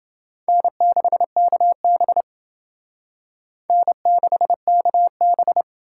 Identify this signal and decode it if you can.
Morse code, heard as a keyed tone, about 28 wpm, 700 Hz. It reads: N6KB N6KB